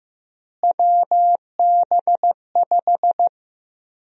Morse code, heard as a keyed tone, about 15 wpm, 700 Hz. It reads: WB5
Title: USA Call Sign Prefixes - Repeated 2x - 15wpm